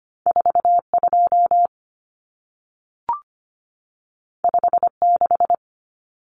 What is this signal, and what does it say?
Morse code, heard as a keyed tone, about 25 wpm, 700 Hz. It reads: 42 56
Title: Numbers - 2-Digit 25wpm